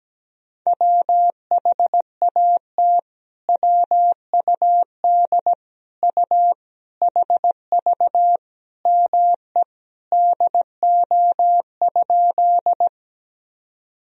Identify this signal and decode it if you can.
Morse code, heard as a keyed tone, about 17 wpm, 700 Hz. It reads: WHAT WUD U HV ME DO?
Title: Sentences from Top 200 Words and Abbreviations 17wpm